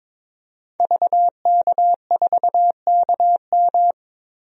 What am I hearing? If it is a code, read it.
Morse code, heard as a keyed tone, about 21 wpm, 700 Hz. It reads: VK4KM
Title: DXCC-100 Call Signs - 5 Characters - 22wpm